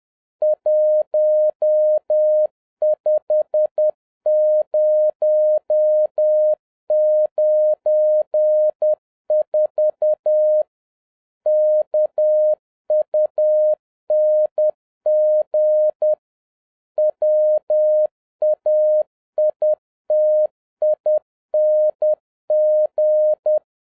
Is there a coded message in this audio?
Yes: Morse code from 0.4 to 23.6 s, heard as a keyed tone, about 10 wpm, 610 Hz. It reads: 15094 KUNG WAITING